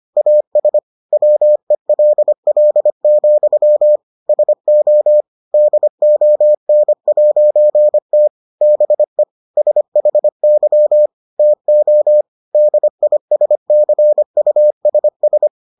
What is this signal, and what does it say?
Morse code, heard as a keyed tone, about 25 wpm, 600 Hz. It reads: AS WELL, SO DON'T BE SHY TO DISCUSS